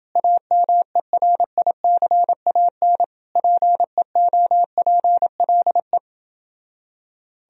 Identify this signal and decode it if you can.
Morse code, heard as a keyed tone, about 29 wpm, 700 Hz. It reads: AMERICAN PEOPLE